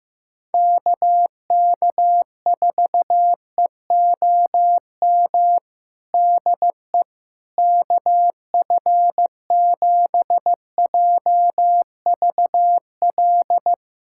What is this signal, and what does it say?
Morse code, heard as a keyed tone, about 15 wpm, 700 Hz. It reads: KK4EOM DE KF7JVL
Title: USA Callsign DE Callsign - 6 Characters - 15wpm